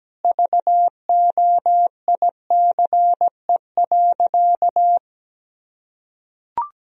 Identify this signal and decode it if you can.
Morse code, heard as a keyed tone, about 17 wpm, 700 Hz. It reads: VOICE.
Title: Sentences from Top 1000 Words - Review Part 4 - 17wpm